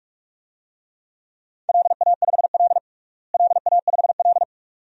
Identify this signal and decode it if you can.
Morse code, heard as a keyed tone, about 41 wpm, 700 Hz. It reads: LA5L LA5L
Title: DXCC-100 Call Signs - 4 Characters - Repeated 2x - 45wpm